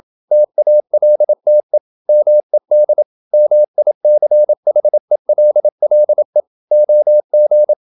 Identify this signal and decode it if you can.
Morse code, heard as a keyed tone, about 27 wpm, 600 Hz. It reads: TALTE MED MICHELLE OG
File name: Ferie QRQ 27wpm